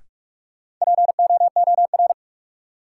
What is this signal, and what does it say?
Morse code, heard as a keyed tone, about 45 wpm, 700 Hz. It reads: POOR
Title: Sentences from Top 1000 Words - New Word Drill - 45wpm